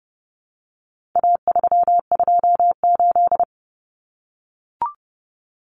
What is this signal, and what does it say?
Morse code, heard as a keyed tone, about 30 wpm, 700 Hz. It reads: A328
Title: Common Cut Numbers - 4-Digit 30wpm